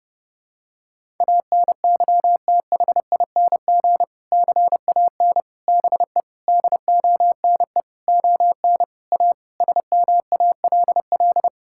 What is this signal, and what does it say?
Morse code, heard as a keyed tone, about 32 wpm, 700 Hz. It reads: ANYTHING CAN BE DONE ON A SMALL
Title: Sentences from Top 1000 Words - Review Part 5 - 30wpm